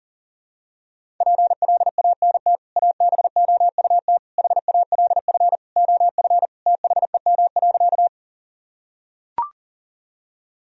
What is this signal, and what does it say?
Morse code, heard as a keyed tone, about 40 wpm, 700 Hz. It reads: PLANT ABOUT HALF OF THEM.
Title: Sentences from Top 1000 Words - Review Part 1 - 40wpm